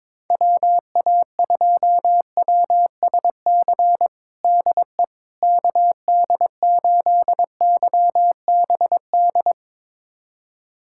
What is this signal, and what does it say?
Morse code, heard as a keyed tone, about 22 wpm, 700 Hz. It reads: WA2WSC DE KD8YBD
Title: USA Callsign DE Callsign - 6 Characters - 22wpm